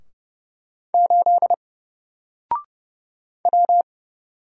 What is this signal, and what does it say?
Morse code, heard as a keyed tone, about 30 wpm, 700 Hz. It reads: 8 W
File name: Single-Letter-Number-30wpm